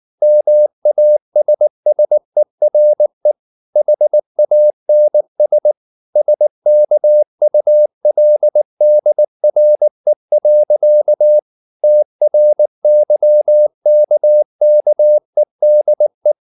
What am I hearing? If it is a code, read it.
Morse code, heard as a keyed tone, about 19 wpm, 600 Hz. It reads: MASSERE HANS SKULDRE. TRYKKEDE
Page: Ferie 19 wpm